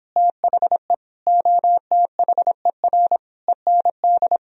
Morse code, heard as a keyed tone, about 26 wpm, 700 Hz. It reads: THE OTHER END